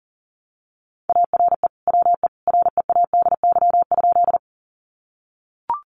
Morse code, heard as a keyed tone, about 37 wpm, 700 Hz. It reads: ARE WE READY?
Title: Sentences from Top 400 Words 40wpm